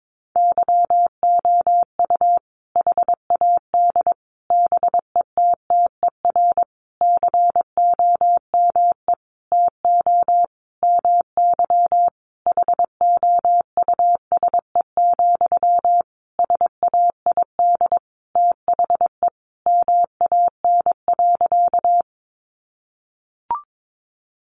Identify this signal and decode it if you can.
Morse code, heard as a keyed tone, about 22 wpm, 700 Hz. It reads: YOU HAD BETTER COME TO MY HOUSE, SAID THE MAN.
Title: Sentences from Top 400 Words 22wpm